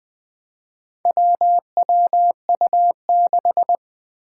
Morse code, heard as a keyed tone, about 20 wpm, 700 Hz. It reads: WWU6
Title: Four Challenging Letter-Number Combinations 20wpm